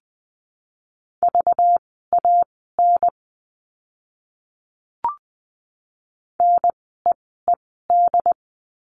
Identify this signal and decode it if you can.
Morse code, heard as a keyed tone, about 20 wpm, spaced as for 15 wpm, 700 Hz. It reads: VAN NEED
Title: 192 Lesson 31f - Letter v - Words 20wpm